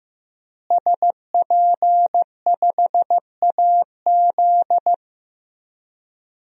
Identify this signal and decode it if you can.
Morse code, heard as a keyed tone, about 15 wpm, 700 Hz. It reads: SP5AZ
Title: DXCC-100 Call Signs - 5 Characters - Repeated 3x - 15wpm